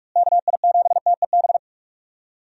Morse code, heard as a keyed tone, about 43 wpm, 700 Hz. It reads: KI7TEB
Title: USA Call Signs - 6 Characters - Repeated 3x - 45wpm